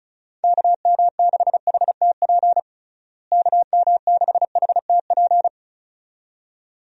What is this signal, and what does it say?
Morse code, heard as a keyed tone, about 36 wpm, 700 Hz. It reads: KM6HTP KM6HTP
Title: USA Call Signs - 6 Characters - Repeated 2x - 35wpm